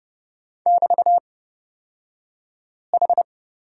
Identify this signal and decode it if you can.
Morse code, heard as a keyed tone, about 30 wpm, spaced as for 5 wpm, 700 Hz. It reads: <BT>H
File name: Lesson 37d - Letter x - Two Characters 30wpm